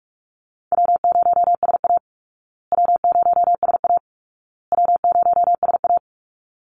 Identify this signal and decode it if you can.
Morse code, heard as a keyed tone, about 44 wpm, 700 Hz. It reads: W0SA W0SA W0SA
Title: Easy Call Signs Repeated 3x - 45wpm